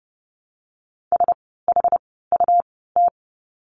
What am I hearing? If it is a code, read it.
Morse code, heard as a keyed tone, about 30 wpm, spaced as for 17 wpm, 700 Hz. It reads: SHUT
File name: Lesson 17f - Letter u - Words 30wpm